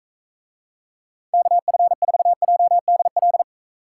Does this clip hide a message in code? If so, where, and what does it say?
Yes: Morse code from 1.3 to 3.4 s, heard as a keyed tone, about 43 wpm, 700 Hz. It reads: KF4JDL